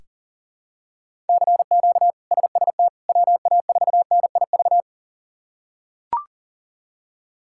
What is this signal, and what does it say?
Morse code, heard as a keyed tone, about 40 wpm, 700 Hz. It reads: CQ SST WA4NIV
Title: K1USN SST QSO Practice - 40wpm